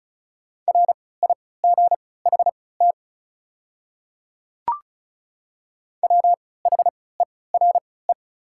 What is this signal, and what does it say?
Morse code, heard as a keyed tone, about 36 wpm, spaced as for 20 wpm, 690 Hz. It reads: RIGHT WHERE
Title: 3x Character Spacing - Top 200 Words - 35wpm